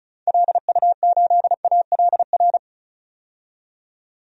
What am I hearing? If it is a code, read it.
Morse code, heard as a keyed tone, about 36 wpm, 700 Hz. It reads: LU8ALR
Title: DXCC-100 Call Signs - 6 Characters - 35wpm